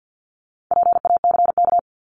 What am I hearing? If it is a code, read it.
Morse code, heard as a keyed tone, about 50 wpm, 700 Hz. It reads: PACK
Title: Top 1001-1250 Words 50wpm